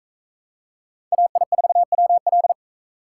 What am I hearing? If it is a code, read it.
Morse code, heard as a keyed tone, about 41 wpm, 700 Hz. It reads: AI4WL